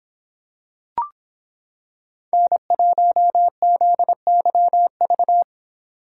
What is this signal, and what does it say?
Morse code, heard as a keyed tone, about 25 wpm, 700 Hz. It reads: N1ZYV